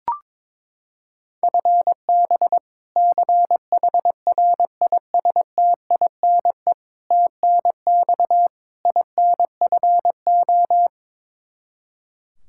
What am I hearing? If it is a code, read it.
Morse code, heard as a keyed tone, about 22 wpm, 700 Hz. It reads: FB CHRISTINE TNX INFO
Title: Practice QSO - Exchange 3 - More Info - 22wpm